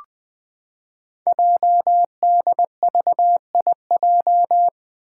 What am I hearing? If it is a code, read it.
Morse code, heard as a keyed tone, about 20 wpm, 700 Hz. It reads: JDVIJ
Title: Code Groups - 5 Letter:Numbers - 20wpm